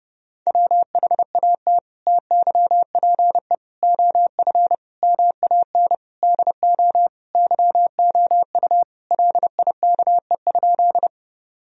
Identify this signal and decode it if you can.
Morse code, heard as a keyed tone, about 31 wpm, 700 Hz. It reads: WHAT TYPE OF MAN DO YOU LIKE?
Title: Sentences from Top 1000 Words - Review Part 3 - 30wpm